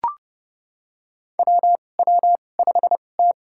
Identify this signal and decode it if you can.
Morse code, heard as a keyed tone, about 30 wpm, spaced as for 22 wpm, 700 Hz. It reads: WW5T
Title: USA Call Signs - 4 Characters - 2x Character Spacing - 30wpm